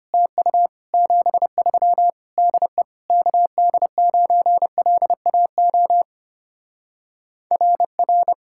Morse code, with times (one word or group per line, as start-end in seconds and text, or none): 0.1-0.7 TU
0.9-2.1 73
2.4-2.8 DE
3.1-6.0 KD9LAO
7.5-8.3 RR